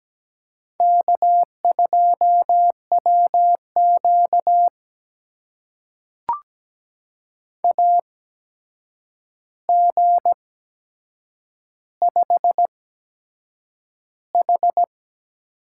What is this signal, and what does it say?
Morse code, heard as a keyed tone, about 17 wpm, spaced as for 7 wpm, 700 Hz. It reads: K2WQ AG5H
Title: USA Call Signs - 4 Characters - 8x Character Spacing - 17wpm